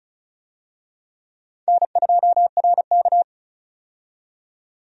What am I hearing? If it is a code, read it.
Morse code, heard as a keyed tone, about 36 wpm, 700 Hz. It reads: N2RK
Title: USA Call Signs - 4 Characters - 35wpm